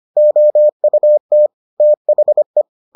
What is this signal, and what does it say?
Morse code, heard as a keyed tone, about 25 wpm, 600 Hz. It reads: OUT THE